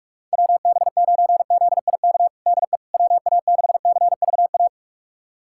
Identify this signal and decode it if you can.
Morse code, heard as a keyed tone, about 46 wpm, 700 Hz. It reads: WB9ZIK DE WA6CVA
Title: USA Callsign DE Callsign - 6 Characters - 45wpm